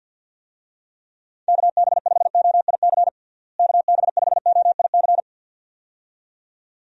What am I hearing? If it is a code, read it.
Morse code, heard as a keyed tone, about 51 wpm, 700 Hz. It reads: KB5OIC KB5OIC
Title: USA Call Signs - 6 Characters - Repeated 2x - 50wpm